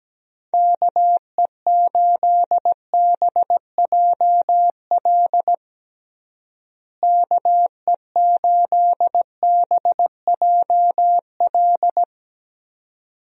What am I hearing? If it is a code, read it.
Morse code, heard as a keyed tone, about 17 wpm, 700 Hz. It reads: KE8BJL KE8BJL
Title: USA Call Signs - 6 Characters - Repeated 2x - 17wpm